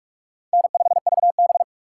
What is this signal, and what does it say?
Morse code, heard as a keyed tone, about 44 wpm, 700 Hz. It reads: N5VB
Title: USA Call Signs - 4 Characters - 4x Character Spacing - 45wpm